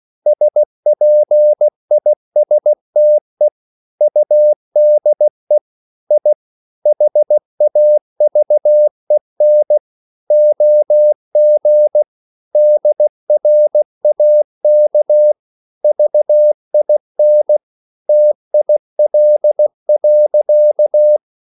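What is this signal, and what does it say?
Morse code, heard as a keyed tone, about 16 wpm, 600 Hz. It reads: SPISTE UDE I HAVEN OG DRAK VIN TIL.
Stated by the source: Slow (QRS) Morse code at 16 wpm, from CW med Gnister.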